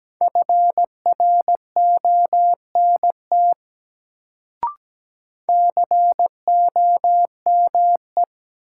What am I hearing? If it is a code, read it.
Morse code, heard as a keyed tone, about 17 wpm, 700 Hz. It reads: FRONT COME
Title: Sets of 2 Words - New Word Drill - 17wpm